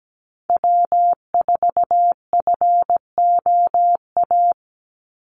Morse code, heard as a keyed tone, about 17 wpm, 700 Hz. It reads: W4FOA
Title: Call Signs Repeated 3x - 17wpm